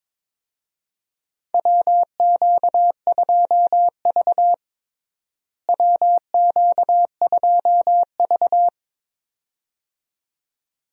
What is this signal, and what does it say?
Morse code, heard as a keyed tone, about 22 wpm, 700 Hz. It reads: WQ2V WQ2V
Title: USA Call Signs - 4 Characters - Repeated 2x - 22wpm